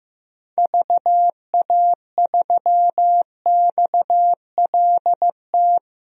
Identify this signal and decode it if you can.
Morse code, heard as a keyed tone, about 15 wpm, 700 Hz. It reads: VA3XLT
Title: DXCC-100 Call Signs - 6 Characters - Repeated 3x - 15wpm